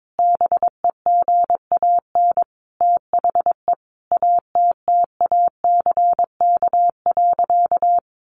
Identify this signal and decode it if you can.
Morse code, heard as a keyed tone, about 22 wpm, 700 Hz. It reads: BEGAN THE ATTACK.
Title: Sentences from Top 1500 Words 22wpm